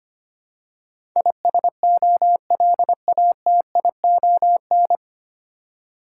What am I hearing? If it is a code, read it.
Morse code, heard as a keyed tone, about 25 wpm, 690 Hz. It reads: ISOLATION
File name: practice / 25wpm / 9 Letter Words - First 3 Letters Given - 25wpm